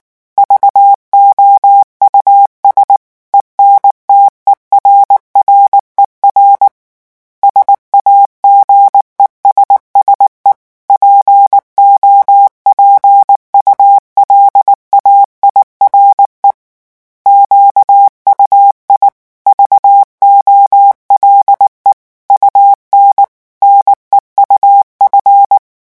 Morse code, heard as a keyed tone, about 19 wpm, 800 Hz. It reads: VOUS ENTERRER SAGESSE POPULAIRE QUI VOLE UN NEUF
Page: REVISION 8 - 8 mots minute